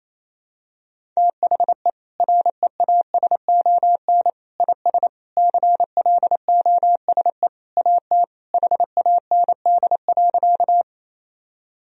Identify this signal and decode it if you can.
Morse code, heard as a keyed tone, about 28 wpm, 700 Hz. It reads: THE REASON IS CLOSE AT HAND.
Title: Sentences from Top 1000 Words - Review Part 1 - 28wpm